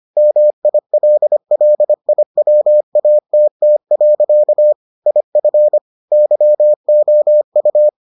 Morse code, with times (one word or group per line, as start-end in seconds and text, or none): 0.2-4.7 MILLIWATT.
5.1-5.8 IF
6.1-7.9 YOU